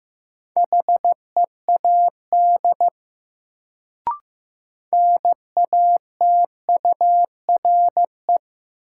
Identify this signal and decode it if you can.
Morse code, heard as a keyed tone, about 15 wpm, 700 Hz. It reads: HEAD NATURE
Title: Sets of 3 Words - New Word Drill - 15wpm